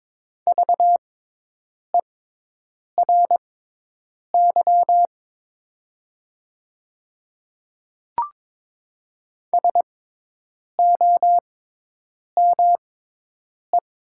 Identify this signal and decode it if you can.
Morse code, heard as a keyed tone, about 22 wpm, spaced as for 8 wpm, 700 Hz. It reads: VERY SOME
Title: 6x Character Spacing - Top 200 Words - 22wpm